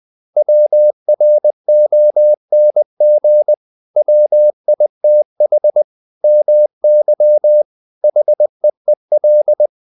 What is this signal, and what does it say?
Morse code, heard as a keyed tone, about 20 wpm, 600 Hz. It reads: WRONG WITH MY HEEL
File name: the old man and the sea part 2 20wpm